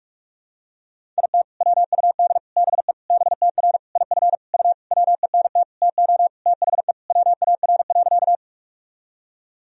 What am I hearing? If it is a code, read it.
Morse code, heard as a keyed tone, about 46 wpm, 700 Hz. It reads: IT WUD BE BTR IF U WENT TO THE WAR.
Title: Sentences from Top 400 Words and Abbreviations 45wpm